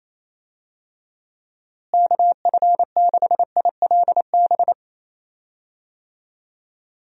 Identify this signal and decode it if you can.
Morse code, heard as a keyed tone, about 28 wpm, 700 Hz. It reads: KF6ILB
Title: USA Call Signs - 6 Characters - Repeated 2x - 28wpm